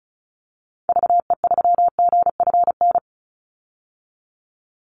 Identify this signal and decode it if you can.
Morse code, heard as a keyed tone, about 34 wpm, 700 Hz. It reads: VE3GFN
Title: Call Signs Repeated 2x - 35wpm